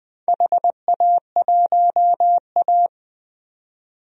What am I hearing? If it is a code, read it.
Morse code, heard as a keyed tone, about 20 wpm, 700 Hz. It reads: HA1A
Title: DXCC-100 Call Signs - 4 Characters - 20wpm